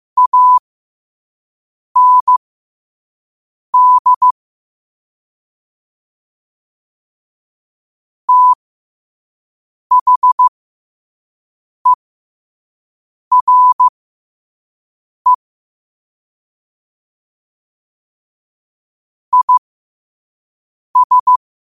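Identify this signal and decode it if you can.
Morse code, heard as a keyed tone, about 15 wpm, spaced as for 5 wpm, 1000 Hz. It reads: AND THERE IS
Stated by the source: Morse at 5 wpm.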